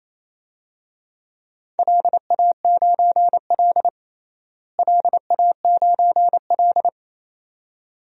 Morse code, heard as a keyed tone, about 28 wpm, 700 Hz. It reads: LA9L LA9L
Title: DXCC-100 Call Signs - 4 Characters - Repeated 2x - 28wpm